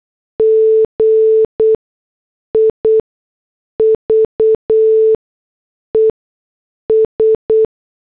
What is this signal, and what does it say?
Morse code, heard as a keyed tone, about 8 wpm, spaced as for 6 wpm, 440 Hz. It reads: GIVES